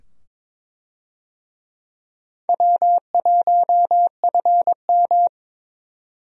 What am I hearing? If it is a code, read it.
Morse code, heard as a keyed tone, about 22 wpm, 700 Hz. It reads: W1FM
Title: USA Call Signs - 4 Characters - 22wpm